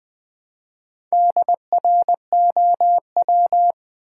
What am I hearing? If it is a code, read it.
Morse code, heard as a keyed tone, about 20 wpm, 700 Hz. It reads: DROW
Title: Build A Word - 1 Letter At A Time - 5-Letter Words - 20wpm